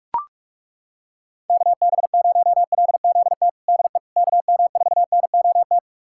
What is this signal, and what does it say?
Morse code, heard as a keyed tone, about 41 wpm, 700 Hz. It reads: KB0LZT DE KM4NOT
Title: USA Callsign DE Callsign - 6 Characters - 45wpm